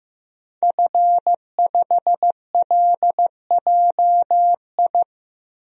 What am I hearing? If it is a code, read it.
Morse code, heard as a keyed tone, about 15 wpm, 700 Hz. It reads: F5LJI
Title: DXCC-100 Call Signs - 5 Characters - Repeated 3x - 15wpm